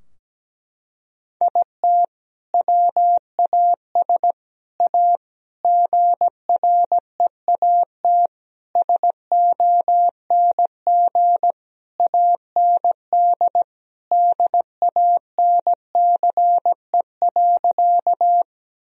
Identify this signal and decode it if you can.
Morse code, heard as a keyed tone, about 17 wpm, 700 Hz. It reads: IT WAS A GREAT SONG AND DANCE.
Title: Sentences from Top 1000 Words - Review Part 4 - 17wpm